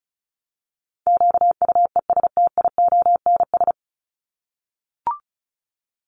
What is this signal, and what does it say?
Morse code, heard as a keyed tone, about 34 wpm, 700 Hz. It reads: QUESTIONS
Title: ES:S Suffix - Top 100 Words with Base Words 35wpm